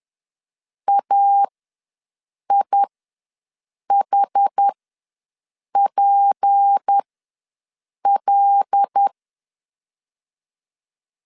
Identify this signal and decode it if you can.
Morse code, heard as a keyed tone, about 10 wpm, spaced as for 6 wpm, 800 Hz. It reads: AIHPL
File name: LEZIONI / 6 PKX0 / 6 P ws 10wpm 3751